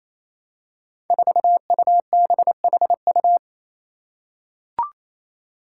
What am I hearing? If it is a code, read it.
Morse code, heard as a keyed tone, about 27 wpm, 700 Hz. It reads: 4UBHU
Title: Five Challenging Letter-Number Combinations 28wpm